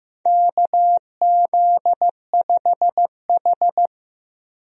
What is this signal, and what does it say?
Morse code, heard as a keyed tone, about 15 wpm, 700 Hz. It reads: KZ5H
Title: USA Call Signs - 4 Characters - 15wpm